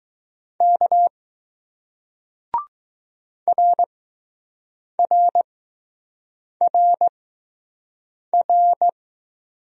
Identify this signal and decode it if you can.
Morse code, heard as a keyed tone, about 20 wpm, spaced as for 6 wpm, 700 Hz. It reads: K RRRR